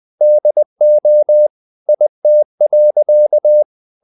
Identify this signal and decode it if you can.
Morse code, heard as a keyed tone, about 20 wpm, 600 Hz. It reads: DO IT.
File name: the old man and the sea part 2 20wpm